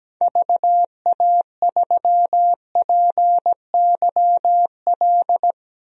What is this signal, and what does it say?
Morse code, heard as a keyed tone, about 17 wpm, 700 Hz. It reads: VA3PYL
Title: DXCC-100 Call Signs - 6 Characters - Repeated 3x - 17wpm